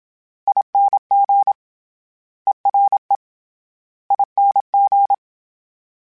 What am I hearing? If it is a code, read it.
Morse code, heard as a keyed tone, about 27 wpm, 800 Hz. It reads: ING ERE ING
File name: Special INGs-mp3s / Special INGs-mp3s / Ere ing @25